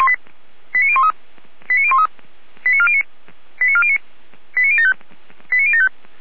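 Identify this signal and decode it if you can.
Five-tone selective call (ZVEI), heard as quick runs of five tones, tones of about 70 ms. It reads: 89013 89013 89409 89409 89075 89075
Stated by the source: ZVEI five-tone call received by SDR.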